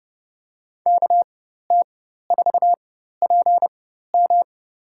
Morse code, heard as a keyed tone, about 32 wpm, spaced as for 14 wpm, 700 Hz. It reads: KT4PM
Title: USA Call Signs - 5 Characters - 4x Character Spacing - 30wpm